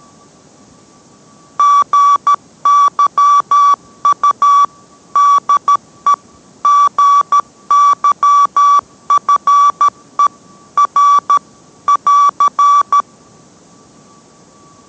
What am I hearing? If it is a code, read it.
Morse code, heard as a keyed tone, about 13 wpm, 1150 Hz. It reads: GYU DE GYFE R <AR>